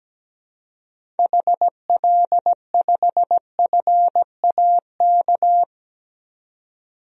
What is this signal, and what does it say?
Morse code, heard as a keyed tone, about 17 wpm, 700 Hz. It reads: HL5FAK
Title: DXCC-100 Call Signs - 6 Characters - 17wpm